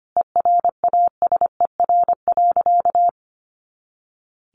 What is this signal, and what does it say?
Morse code, heard as a keyed tone, about 25 wpm, 700 Hz. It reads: ERASER.
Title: Additional Easy Phrases 25wpm